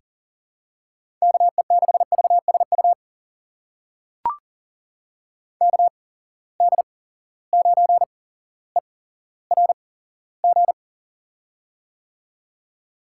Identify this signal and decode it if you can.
Morse code, heard as a keyed tone, about 37 wpm, spaced as for 17 wpm, 700 Hz. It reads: KE6VSU KD9ERG
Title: USA Call Signs - 6 Characters - 8x Character Spacing - 40wpm